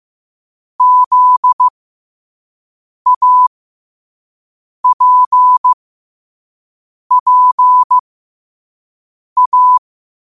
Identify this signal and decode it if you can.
Morse code, heard as a keyed tone, about 15 wpm, spaced as for 5 wpm, 1000 Hz. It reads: ZAPPA
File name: qotdfiles / 5wpm